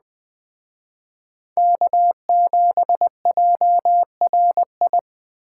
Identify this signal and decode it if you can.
Morse code, heard as a keyed tone, about 21 wpm, 700 Hz. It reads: K7JRI
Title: Speed Racing - USA 1x3 Call Signs 17wpm